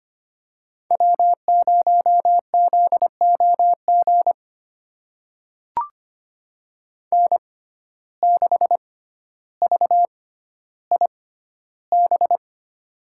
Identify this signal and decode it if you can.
Morse code, heard as a keyed tone, about 25 wpm, spaced as for 13 wpm, 700 Hz. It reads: W0ZOG N6VIB
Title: USA Call Signs - 5 Characters - 6x Character Spacing - 25wpm